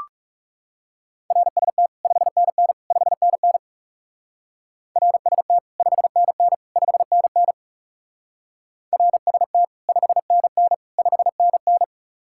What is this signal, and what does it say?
Morse code, heard as a keyed tone, about 35 wpm, 700 Hz. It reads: RST 5NN 5NN RST 5NN 5NN RST 5NN 5NN
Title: Speed-Racing Common QSO Phrases Encore - 30wpm